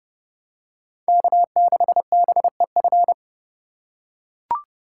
Morse code, heard as a keyed tone, about 26 wpm, 700 Hz. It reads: K6BEF